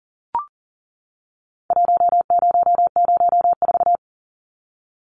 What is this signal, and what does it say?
Morse code, heard as a keyed tone, about 38 wpm, 700 Hz. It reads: 1004
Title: Numbers - 4-Digit 40wpm